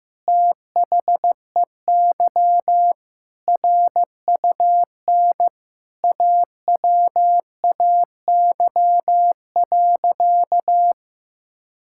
Morse code, heard as a keyed tone, about 15 wpm, 700 Hz. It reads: THEY RUN AWAY.
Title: Sentences from Top 1600 Words 15wpm